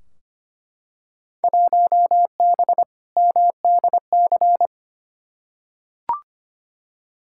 Morse code, heard as a keyed tone, about 25 wpm, 700 Hz. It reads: 1B MDC
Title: Field Day Exchanges - 25wpm